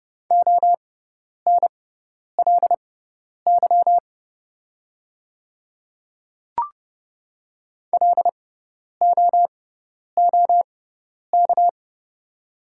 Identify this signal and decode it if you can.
Morse code, heard as a keyed tone, about 30 wpm, spaced as for 10 wpm, 690 Hz. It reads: ONLY LOOK
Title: 6x Character Spacing - Top 500 Words - 30wpm